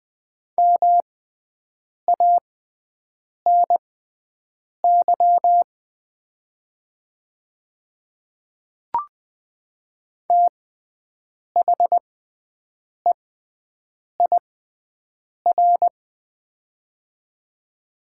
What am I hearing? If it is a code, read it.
Morse code, heard as a keyed tone, about 20 wpm, spaced as for 7 wpm, 700 Hz. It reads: MANY THEIR
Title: 6x Character Spacing - Top 100 Words - 20wpm